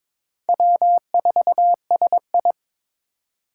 Morse code, heard as a keyed tone, about 22 wpm, 700 Hz. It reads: W4SI
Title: USA Call Signs - 4 Characters - 3x Character Spacing - 22wpm